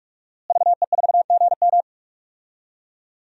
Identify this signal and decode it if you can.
Morse code, heard as a keyed tone, about 46 wpm, 700 Hz. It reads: VE4GM
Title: DXCC-100 Call Signs - 5 Characters - Repeated 3x - 45wpm